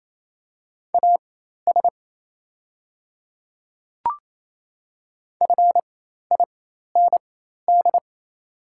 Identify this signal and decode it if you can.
Morse code, heard as a keyed tone, about 28 wpm, spaced as for 13 wpm, 700 Hz. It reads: AS FIND
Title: 4x Character Spacing - Top 200 Words - 28wpm